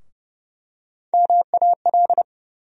Morse code, heard as a keyed tone, about 30 wpm, 700 Hz. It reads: MAL
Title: USA Call Sign Postfixes - 3 Letters 30wpm